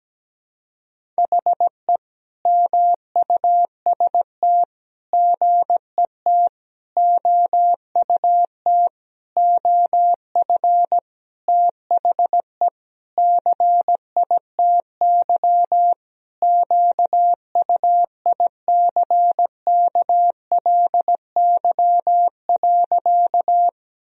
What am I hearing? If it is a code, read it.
Morse code, heard as a keyed tone, about 17 wpm, 700 Hz. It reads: HE MUST GET OUT OF THE CITY QUICKLY.
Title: Sentences from Top 1000 Words - Review Part 1 - 17wpm